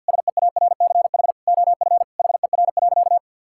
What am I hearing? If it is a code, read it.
Morse code, heard as a keyed tone, about 51 wpm, 700 Hz. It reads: SEARCH OF HER.
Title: Sentences from Top 1000 Words - Review Part 5 - 50wpm